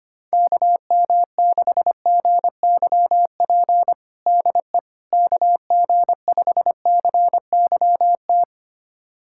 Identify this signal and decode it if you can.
Morse code, heard as a keyed tone, about 25 wpm, 700 Hz. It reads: KM6GYP DE KG5CYT
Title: USA Callsign DE Callsign - 6 Characters - 25wpm